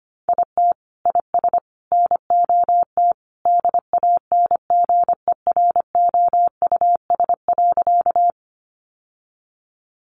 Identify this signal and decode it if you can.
Morse code, heard as a keyed tone, about 25 wpm, 700 Hz. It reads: IT IS NOT DANGEROUS.
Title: Sentences from Top 1200 Words 25wpm